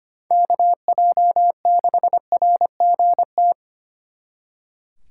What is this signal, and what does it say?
Morse code, heard as a keyed tone, about 31 wpm, 700 Hz. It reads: KJ6RGT
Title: Speed Racing - USA 2x3 Call Signs 25wpm